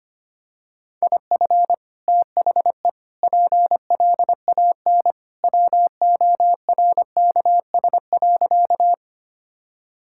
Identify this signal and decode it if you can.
Morse code, heard as a keyed tone, about 25 wpm, 690 Hz. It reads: IF THE PLAN WORKS.
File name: practice / 25wpm / Sentences with 2-Word Phrases - 25wpm